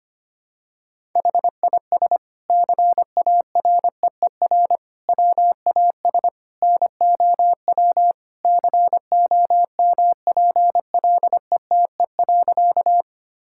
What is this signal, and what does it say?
Morse code, heard as a keyed tone, about 25 wpm, 700 Hz. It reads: HIS CAREER WAS NOW COMPLETE.
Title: Sentences from Top 1000 Words - Review Part 6 - 25wpm